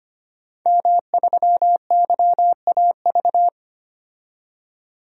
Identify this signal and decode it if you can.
Morse code, heard as a keyed tone, about 25 wpm, 700 Hz. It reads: M3YAV
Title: DXCC-100 Call Signs - 5 Characters - 25wpm